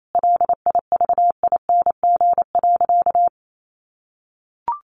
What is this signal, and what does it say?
Morse code, heard as a keyed tone, about 28 wpm, 700 Hz. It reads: LIVING.
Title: Sentences from Top 1400 Words 28wpm